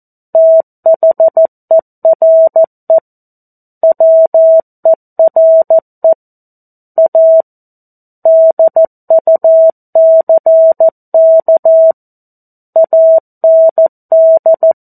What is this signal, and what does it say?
Morse code, heard as a keyed tone, about 14 wpm, 650 Hz. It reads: THERE WERE A DUCK AND
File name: 02-Chapter-02 14 WPM